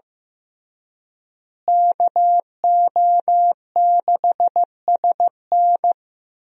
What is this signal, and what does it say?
Morse code, heard as a keyed tone, about 15 wpm, 700 Hz. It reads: KO6SN
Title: USA Call Signs - 5 Characters - Repeated 3x - 15wpm